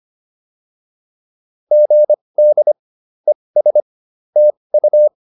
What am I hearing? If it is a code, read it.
Morse code, heard as a keyed tone, about 25 wpm, spaced as for 20 wpm, 600 Hz. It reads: GD ES TU